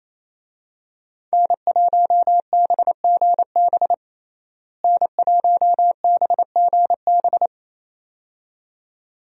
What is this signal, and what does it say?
Morse code, heard as a keyed tone, about 28 wpm, 700 Hz. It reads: N1BGB N1BGB
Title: USA Call Signs - 5 Characters - Repeated 2x - 28wpm